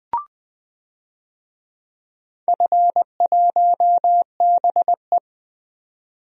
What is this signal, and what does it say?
Morse code, heard as a keyed tone, about 20 wpm, 700 Hz. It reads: F1BE
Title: DXCC-100 Call Signs - 4 Characters - 20wpm